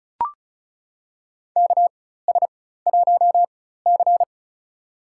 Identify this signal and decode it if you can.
Morse code, heard as a keyed tone, about 34 wpm, spaced as for 16 wpm, 700 Hz. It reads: KS1C
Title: USA Call Signs - 4 Characters - 4x Character Spacing - 35wpm